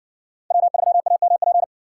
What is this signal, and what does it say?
Morse code, heard as a keyed tone, about 57 wpm, 700 Hz. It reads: W3AMP